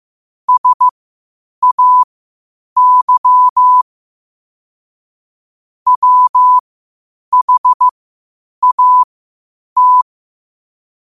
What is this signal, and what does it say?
Morse code, heard as a keyed tone, about 15 wpm, spaced as for 8 wpm, 1000 Hz. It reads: SAY WHAT
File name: qotdfiles / 8wpm